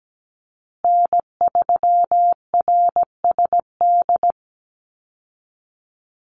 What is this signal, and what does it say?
Morse code, heard as a keyed tone, about 17 wpm, 700 Hz. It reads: N3RSD
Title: Call Signs Repeated 3x - 17wpm